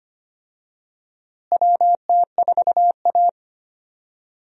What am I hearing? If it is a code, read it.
Morse code, heard as a keyed tone, about 25 wpm, 700 Hz. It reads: WT4A
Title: USA Call Signs - 4 Characters - Repeated 3x - 25wpm